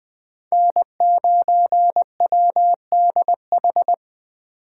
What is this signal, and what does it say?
Morse code, heard as a keyed tone, about 25 wpm, 700 Hz. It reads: N9WDH
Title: Speed Racing - USA 1x3 Call Signs 20wpm